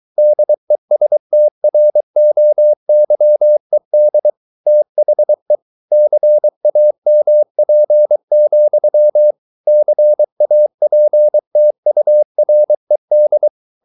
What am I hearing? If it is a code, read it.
Morse code, heard as a keyed tone, about 23 wpm, 600 Hz. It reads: DESTROYED THE CAMP, CAPTURED